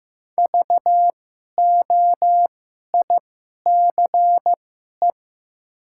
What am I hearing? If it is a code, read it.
Morse code, heard as a keyed tone, about 15 wpm, spaced as for 11 wpm, 700 Hz. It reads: VOICE
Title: 2x Character Spacing - Top 500 Words - 15wpm